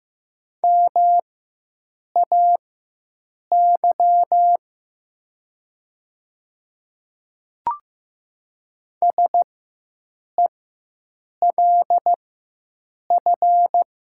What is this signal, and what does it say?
Morse code, heard as a keyed tone, about 15 wpm, spaced as for 7 wpm, 700 Hz. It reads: MAY SELF
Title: 4x Character Spacing - Top 200 Words - 15wpm